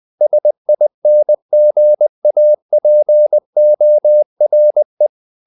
Morse code, heard as a keyed tone, about 20 wpm, 600 Hz. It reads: SINGAPORE
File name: CW / Practice / NationalCapitals20